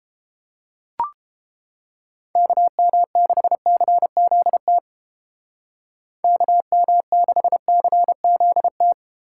Morse code, heard as a keyed tone, about 31 wpm, 700 Hz. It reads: KM6CZT KM6CZT